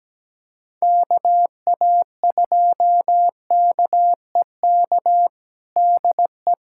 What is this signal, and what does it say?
Morse code, heard as a keyed tone, about 17 wpm, 700 Hz. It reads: KA2KEK DE
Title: USA Callsign DE Callsign - 6 Characters - 17wpm